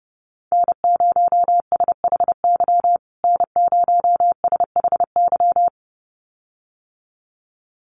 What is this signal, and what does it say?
Morse code, heard as a keyed tone, about 30 wpm, 700 Hz. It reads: N0SHY N0SHY
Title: USA Call Signs Repeated 2x 30wpm